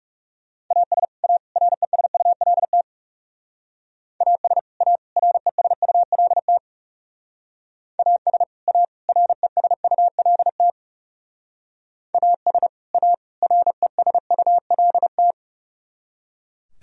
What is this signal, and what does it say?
Morse code, heard as a keyed tone, about 36 wpm, 700 Hz. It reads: AS A RESULT AS A RESULT AS A RESULT AS A RESULT
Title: Speed-Racing - Sets of 3 Words 30wpm